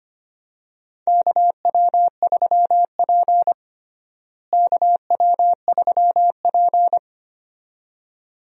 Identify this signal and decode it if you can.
Morse code, heard as a keyed tone, about 25 wpm, 700 Hz. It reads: KW3P KW3P
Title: USA Call Signs - 4 Characters - Repeated 2x - 25wpm